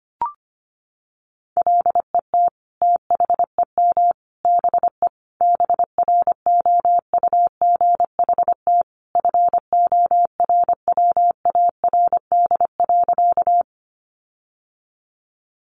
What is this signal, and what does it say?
Morse code, heard as a keyed tone, about 25 wpm, 700 Hz. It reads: LET THEM BE BROUGHT FORWARD.
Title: Sentences from Top 1300 Words 25wpm